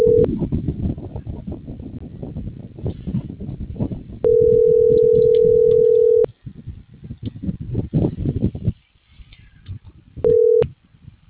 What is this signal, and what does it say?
Ringback tone (North America), 440 + 480 Hz.